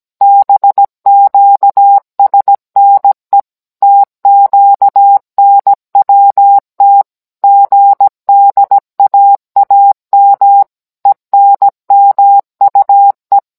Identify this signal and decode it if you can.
Morse code, heard as a keyed tone, about 17 wpm, 800 Hz. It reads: BQSNE TQNWT GDAAM ENMUE